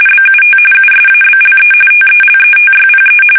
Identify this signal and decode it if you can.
RTTY, about 45.45 baud. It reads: BUT USABLE TELETYPE
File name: fsk850 / teletype-RTTY